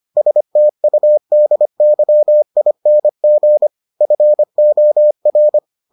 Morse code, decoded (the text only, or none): STUDYING FOR